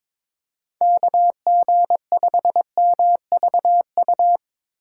Morse code, heard as a keyed tone, about 22 wpm, 700 Hz. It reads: KG5MVU
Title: USA Call Signs - 6 Characters - 8x Character Spacing - 22wpm